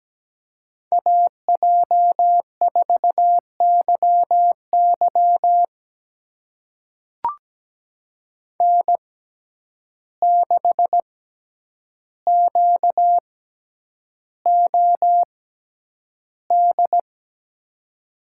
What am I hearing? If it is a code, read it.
Morse code, heard as a keyed tone, about 17 wpm, spaced as for 9 wpm, 700 Hz. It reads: AJ4YY N6QOD
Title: USA Call Signs - 5 Characters - 6x Character Spacing - 17wpm